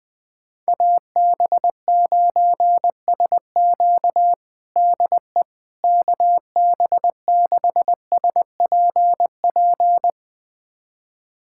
Morse code, heard as a keyed tone, about 20 wpm, 700 Hz. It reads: AB9SQ DE KB6SPP
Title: USA Callsign DE Callsign 20wpm